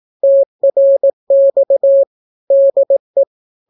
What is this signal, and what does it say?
Morse code, heard as a keyed tone, about 18 wpm, 550 Hz. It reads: TRX DE
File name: naukapodstaw / skroty / skroty